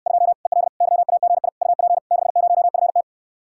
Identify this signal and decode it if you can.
Morse code, heard as a keyed tone, about 69 wpm, 700 Hz. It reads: 2 EL YAGI UP 60FT